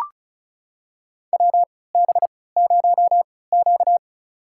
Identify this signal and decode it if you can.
Morse code, heard as a keyed tone, about 36 wpm, spaced as for 20 wpm, 700 Hz. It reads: WB0Q
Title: USA Call Signs - 4 Characters - 3x Character Spacing - 35wpm